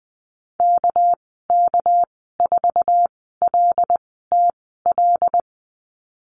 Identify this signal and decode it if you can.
Morse code, heard as a keyed tone, about 20 wpm, spaced as for 14 wpm, 700 Hz. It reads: KK4LTL
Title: 238 Lesson 38g - Letter q - Call Signs 20wpm